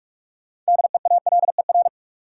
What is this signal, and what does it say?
Morse code, heard as a keyed tone, about 46 wpm, 700 Hz. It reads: DEALER
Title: Sentences from Top 1100 Words - New Word Drill - 45wpm